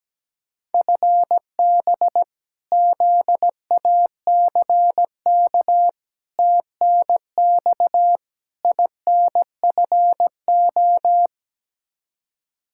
Morse code, heard as a keyed tone, about 17 wpm, 700 Hz. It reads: FB ZACK TNX INFO
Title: Practice QSO - Exchange 3 - More Info - 17wpm